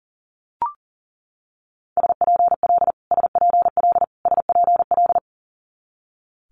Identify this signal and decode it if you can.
Morse code, heard as a keyed tone, about 38 wpm, 690 Hz. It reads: SPL SPL SPL
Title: Multi-letter Phonemes Repeated 3x 40wpm